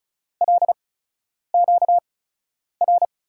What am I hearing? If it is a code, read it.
Morse code, heard as a keyed tone, about 34 wpm, spaced as for 10 wpm, 700 Hz. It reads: LQR
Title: USA Call Signs - 5 Characters - 8x Character Spacing - 35wpm